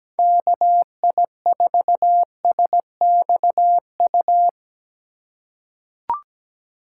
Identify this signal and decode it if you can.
Morse code, heard as a keyed tone, about 17 wpm, 700 Hz. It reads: KI4SXU
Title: USA Call Signs - 6 Characters - 3x Character Spacing - 17wpm